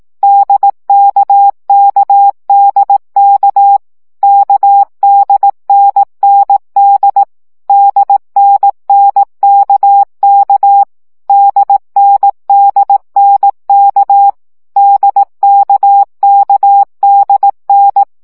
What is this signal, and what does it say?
Morse code, heard as a keyed tone, about 18 wpm, 800 Hz. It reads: DKKDK KDNND DNNKK DNDNK DKKDN
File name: CW gyakorlas n d k 15 baud